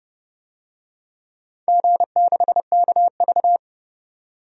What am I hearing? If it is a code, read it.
Morse code, heard as a keyed tone, about 30 wpm, 700 Hz. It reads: G6KV
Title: DXCC-100 Call Signs - 4 Characters - 30wpm